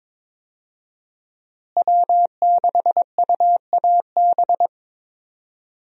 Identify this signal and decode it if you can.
Morse code, heard as a keyed tone, about 21 wpm, 700 Hz. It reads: W6UAB
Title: USA Call Signs - 5 Characters - Repeated 2x - 22wpm